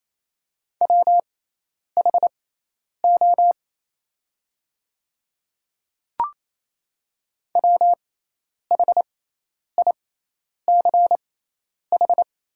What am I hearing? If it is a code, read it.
Morse code, heard as a keyed tone, about 28 wpm, spaced as for 10 wpm, 700 Hz. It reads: WHO WHICH
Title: 6x Character Spacing - Top 100 Words - 28wpm